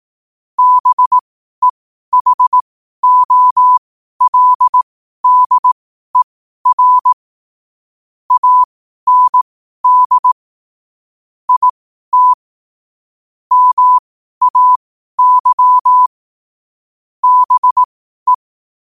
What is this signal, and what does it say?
Morse code, heard as a keyed tone, about 18 wpm, spaced as for 12 wpm, 1000 Hz. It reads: BEHOLDER AND IT MAY BE
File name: qotdfiles / 12wpm